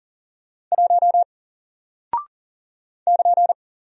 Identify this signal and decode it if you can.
Morse code, heard as a keyed tone, about 40 wpm, 700 Hz. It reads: 1 <KN>
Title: Single Character (Letter:Number:Punctuation:Prosign) 40wpm